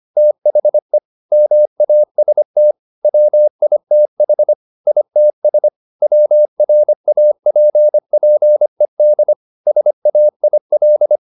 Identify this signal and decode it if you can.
Morse code, heard as a keyed tone, about 25 wpm, 600 Hz. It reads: THE MAST WITH ITS WRAPPED SAIL
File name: the old man and the sea part 1 25wpm